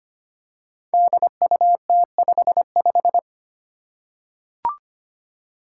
Morse code, heard as a keyed tone, about 25 wpm, 700 Hz. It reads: DUT55
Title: Five Challenging Letter-Number Combinations 25wpm